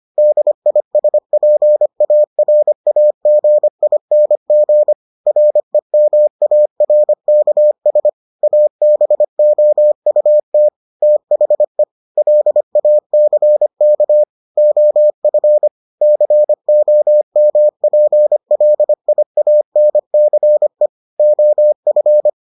Morse code, heard as a keyed tone, about 25 wpm, 600 Hz. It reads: DISPARAGING REMARKS ABOUT THE LACK OF COMPLIANCE OF